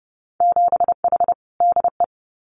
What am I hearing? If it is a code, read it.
Morse code, heard as a keyed tone, about 30 wpm, 700 Hz. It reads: 7H DE
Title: Winter Field Day Exchanges Encore 30wpm